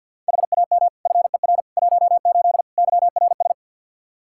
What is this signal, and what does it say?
Morse code, heard as a keyed tone, about 46 wpm, 700 Hz. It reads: HAM FER 18 YRS
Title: Practice QSO - Exchange 3 - More Info - 50wpm